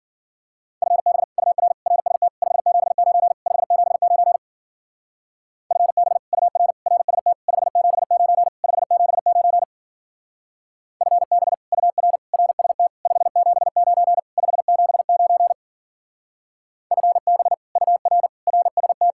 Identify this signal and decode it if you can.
Morse code, heard as a keyed tone, about 48 wpm, 700 Hz. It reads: FB UR RST 579 579 FB UR RST 579 579 FB UR RST 579 579 FB UR RST
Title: Speed-Racing Common QSO Phrases Encore - 40wpm